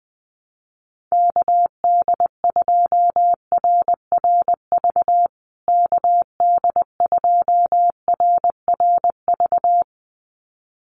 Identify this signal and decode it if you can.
Morse code, heard as a keyed tone, about 20 wpm, 700 Hz. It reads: KD2RRV KD2RRV
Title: USA Call Signs Repeated 2x 20wpm